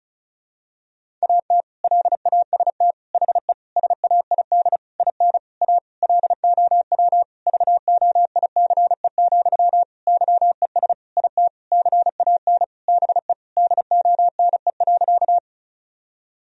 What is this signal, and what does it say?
Morse code, heard as a keyed tone, about 34 wpm, 700 Hz. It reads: AT LAST HE SAID IN A LOW VOICE, YES IT CAN BE DONE.
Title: Sentences from Top 1000 Words - Review Part 1 - 35wpm